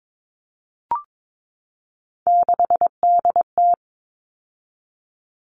Morse code, heard as a keyed tone, about 22 wpm, 700 Hz. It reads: 6DT
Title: All Cut Numbers - 3-Digit 22wpm